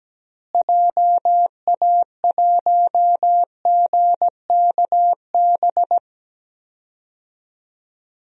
Morse code, heard as a keyed tone, about 17 wpm, 700 Hz. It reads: JA1GKB
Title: DXCC-100 Call Signs - 6 Characters - Repeated 2x - 17wpm